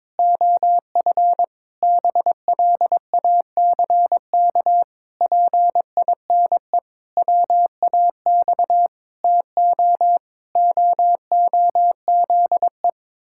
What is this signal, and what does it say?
Morse code, heard as a keyed tone, about 22 wpm, 700 Hz. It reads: OF BLACK PINE WAX TO OOZE
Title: Pangram Challenge v2 - 22wpm